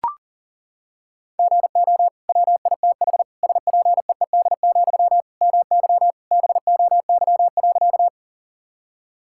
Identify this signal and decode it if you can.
Morse code, heard as a keyed tone, about 37 wpm, 700 Hz. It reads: GO WITH SPEED, MY BOY.
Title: Sentences from Top 1000 Words - Review Part 5 - 40wpm